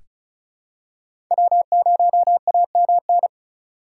Morse code, heard as a keyed tone, about 36 wpm, 700 Hz. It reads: W0AMN
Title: USA Call Signs - 5 Characters - 2x Character Spacing - 35wpm